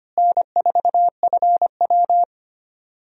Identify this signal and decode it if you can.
Morse code, heard as a keyed tone, about 25 wpm, 700 Hz. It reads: N4FW
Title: USA Call Signs - 4 Characters - Repeated 3x - 25wpm